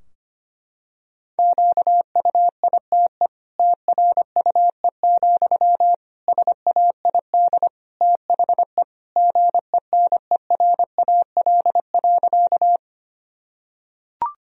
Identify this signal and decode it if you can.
Morse code, heard as a keyed tone, about 25 wpm, 700 Hz. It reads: QUITE TRUE, SAID THE GENERAL.
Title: Sentences from Top 1000 Words - Review Part 3 - 25wpm